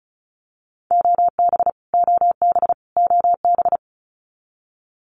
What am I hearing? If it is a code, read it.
Morse code, heard as a keyed tone, about 34 wpm, 700 Hz. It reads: OB OB OB
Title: Abbreviations Repeated 3x - 35wpm